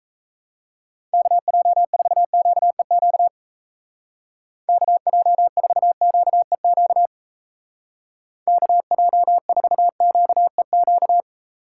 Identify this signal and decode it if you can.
Morse code, heard as a keyed tone, about 34 wpm, 700 Hz. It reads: KJ4QEQ KJ4QEQ KJ4QEQ
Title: Speed Racing - USA 2x3 Call Signs 28wpm